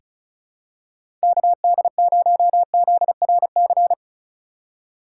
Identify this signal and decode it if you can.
Morse code, heard as a keyed tone, about 34 wpm, 700 Hz. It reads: KD0ZRC
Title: USA Call Signs - 3x Character Spacing - 35wpm